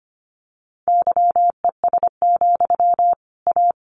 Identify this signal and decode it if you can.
Morse code, heard as a keyed tone, about 25 wpm, 700 Hz. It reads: YES, A
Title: Sentences from Top 1500 Words 25wpm